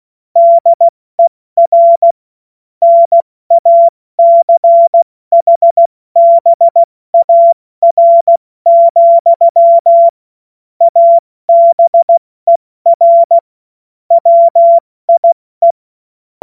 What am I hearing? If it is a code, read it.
Morse code, heard as a keyed tone, about 16 wpm, spaced as for 14 wpm, 680 Hz. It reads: DER NACHBAR, ABER WIE
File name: Morsekurs / Witze / Witz 02. Mai 2023 70 F80